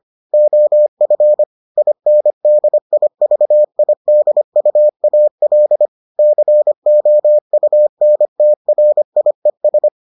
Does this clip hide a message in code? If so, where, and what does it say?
Yes: Morse code from 0.3 to 9.9 s, heard as a keyed tone, about 25 wpm, 600 Hz. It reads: OF INDIVIDUAL COUNTRIES